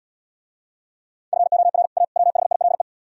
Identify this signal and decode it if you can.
Morse code, heard as a keyed tone, about 78 wpm, 700 Hz. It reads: FOR A WHILE